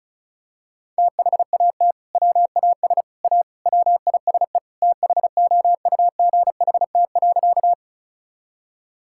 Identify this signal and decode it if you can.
Morse code, heard as a keyed tone, about 36 wpm, 700 Hz. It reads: THAT WAS A WISE THOUGHT.
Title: Sentences from Top 1600 Words 35wpm